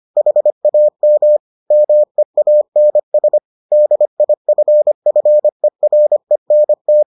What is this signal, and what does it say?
Morse code, heard as a keyed tone, about 25 wpm, 600 Hz. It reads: HAM MEANS DIFFERENT